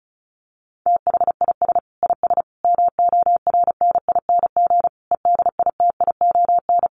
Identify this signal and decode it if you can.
Morse code, heard as a keyed tone, about 34 wpm, 700 Hz. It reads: THIS IS MORNING EDITION
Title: Sets of 5 Words Encore2 35wpm